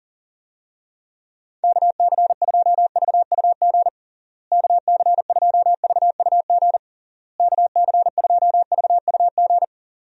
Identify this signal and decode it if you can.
Morse code, heard as a keyed tone, about 40 wpm, 700 Hz. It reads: KC2VUG KC2VUG KC2VUG
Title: USA Call Signs - 6 Characters - Repeated 3x - 40wpm